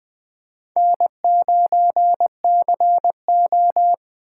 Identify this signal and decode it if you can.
Morse code, heard as a keyed tone, about 20 wpm, 700 Hz. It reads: N9CO
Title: USA Call Signs - 4 Characters - 8x Character Spacing - 20wpm